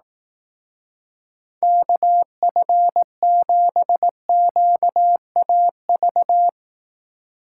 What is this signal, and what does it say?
Morse code, heard as a keyed tone, about 18 wpm, 700 Hz. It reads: KF7QAV
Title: Speed Racing - USA Call Signs 15wpm